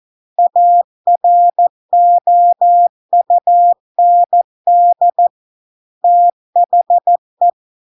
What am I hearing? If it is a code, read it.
Morse code, heard as a keyed tone, about 14 wpm, 700 Hz. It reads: AROUND THE